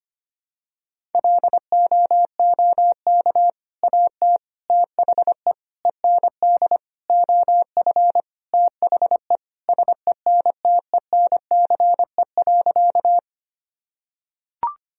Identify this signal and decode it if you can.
Morse code, heard as a keyed tone, about 25 wpm, 700 Hz. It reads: LOOK AT THE END OF THE SENTENCE.
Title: Sentences from Top 1000 Words - Review Part 1 - 25wpm